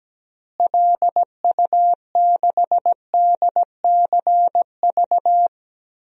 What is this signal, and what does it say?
Morse code, heard as a keyed tone, about 17 wpm, 700 Hz. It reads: LU6DCV
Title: DXCC-100 Call Signs - 6 Characters - Repeated 2x - 17wpm